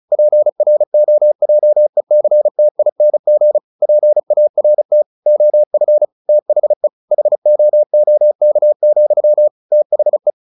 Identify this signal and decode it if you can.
Morse code, heard as a keyed tone, about 35 wpm, 600 Hz. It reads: PROJECTING PART OF THE HOOK, THE
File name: the old man and the sea / the old man and the sea 1 35wpm